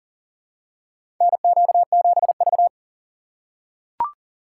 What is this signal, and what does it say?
Morse code, heard as a keyed tone, about 39 wpm, 700 Hz. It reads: NQ7V
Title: USA Call Signs - 4 Characters - 2x Character Spacing - 40wpm